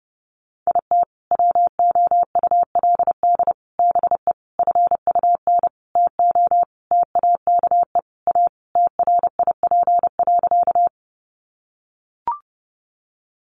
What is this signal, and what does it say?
Morse code, heard as a keyed tone, about 30 wpm, 700 Hz. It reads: IT WOULD BE FUN TO TAKE A TRIP.
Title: Sentences from Top 1300 Words 30wpm